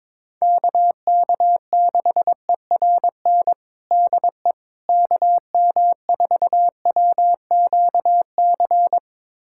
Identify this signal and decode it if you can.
Morse code, heard as a keyed tone, about 21 wpm, 700 Hz. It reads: KK6ERN DE KM4WQC
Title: USA Callsign DE Callsign - 6 Characters - 22wpm